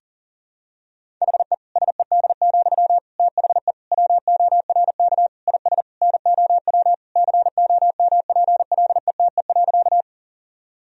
Morse code, heard as a keyed tone, about 40 wpm, 700 Hz. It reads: HE SED, THE WORK IS NOW COMPLETE.
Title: Sentences from Top 400 Words and Abbreviations 40wpm